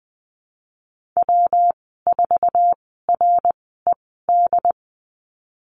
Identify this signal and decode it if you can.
Morse code, heard as a keyed tone, about 20 wpm, spaced as for 15 wpm, 700 Hz. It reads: W4RED
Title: 186 Lesson 30g - Letter b - Call Signs 20wpm